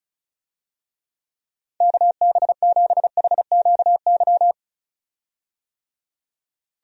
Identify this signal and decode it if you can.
Morse code, heard as a keyed tone, about 36 wpm, 700 Hz. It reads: KB7HQY
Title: USA Call Signs - 6 Characters - 35wpm